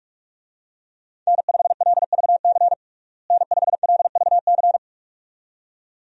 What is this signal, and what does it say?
Morse code, heard as a keyed tone, about 46 wpm, 700 Hz. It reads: N5LVC N5LVC
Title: USA Call Signs - 5 Characters - Repeated 2x - 45wpm